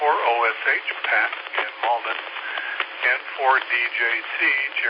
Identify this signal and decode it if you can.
Emergency Alert System (SAME) tones, heard as harsh buzzing bursts, with speech. It is heard from 1.0 to 4.2 s.